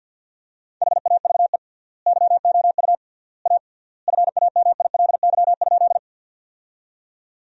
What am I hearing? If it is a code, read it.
Morse code, heard as a keyed tone, about 50 wpm, 700 Hz. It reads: HAVE YOU A FAMILY?
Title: 3x Word-Spacing - Sentences from Top 400 Words - 50wpm